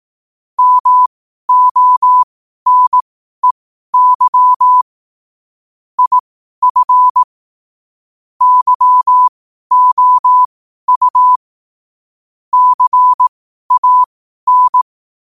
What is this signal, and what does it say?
Morse code, heard as a keyed tone, about 18 wpm, spaced as for 12 wpm, 1000 Hz. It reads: MONEY IF YOU CAN
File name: qotdfiles / 12wpm